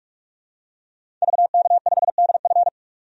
Morse code, heard as a keyed tone, about 47 wpm, 700 Hz. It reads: VK5DF
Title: DXCC-100 Call Signs - 5 Characters - 45wpm